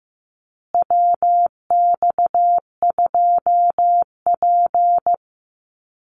Morse code, heard as a keyed tone, about 15 wpm, 700 Hz. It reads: WX2P
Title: USA Call Signs Repeated 3x 15wpm